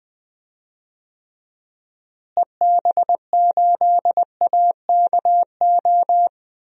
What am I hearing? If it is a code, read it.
Morse code, heard as a keyed tone, about 21 wpm, 700 Hz. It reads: EB8AKO
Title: DXCC-100 Call Signs - 6 Characters - Repeated 3x - 20wpm